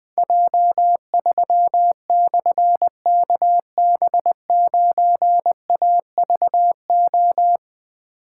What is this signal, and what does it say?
Morse code, heard as a keyed tone, about 20 wpm, 700 Hz. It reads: J3/KB9AVO
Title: Difficult Call Signs 20wpm